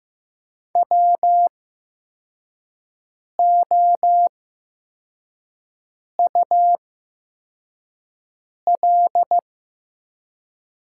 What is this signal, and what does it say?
Morse code, heard as a keyed tone, about 15 wpm, spaced as for 4 wpm, 700 Hz. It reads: WOUL
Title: 8x Character Spacing - Top 100 Words - 15wpm